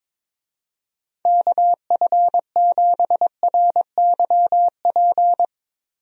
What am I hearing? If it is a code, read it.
Morse code, heard as a keyed tone, about 22 wpm, 700 Hz. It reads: KF7RYP
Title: USA Call Signs - 6 Characters - Repeated 3x - 22wpm